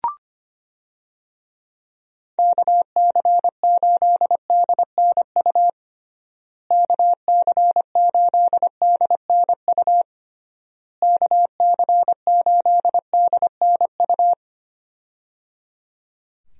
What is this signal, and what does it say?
Morse code, heard as a keyed tone, about 25 wpm, 700 Hz. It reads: KC8DNU KC8DNU KC8DNU
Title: USA Call Signs - 6 Characters - Repeated 3x - 25wpm